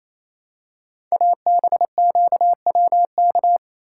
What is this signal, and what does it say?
Morse code, heard as a keyed tone, about 27 wpm, 700 Hz. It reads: ABQWK
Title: Code Groups - 5 Letters - 28wpm